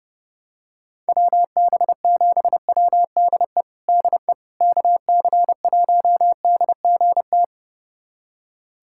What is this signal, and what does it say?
Morse code, heard as a keyed tone, about 30 wpm, 700 Hz. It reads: WB7WDE DE KC1DGT
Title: USA Callsign DE Callsign - 6 Characters - 30wpm